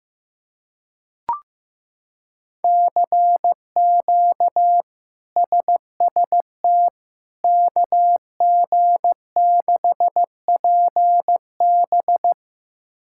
Morse code, heard as a keyed tone, about 15 wpm, 700 Hz. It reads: CQ SST KG6PB
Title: K1USN SST QSO Practice - 15wpm